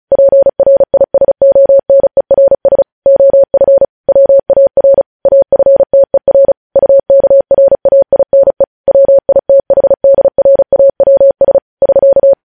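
Morse code, heard as a keyed tone, about 36 wpm, 550 Hz. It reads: PRISONERS OF WAR AFTER UKRAINE WITHDRAWS <SK>
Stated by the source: Morse code news headlines sent at 35.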